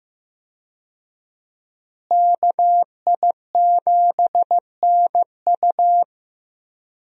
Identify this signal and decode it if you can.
Morse code, heard as a keyed tone, about 15 wpm, 700 Hz. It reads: KI7NU
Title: USA Call Signs - 5 Characters - Repeated 3x - 15wpm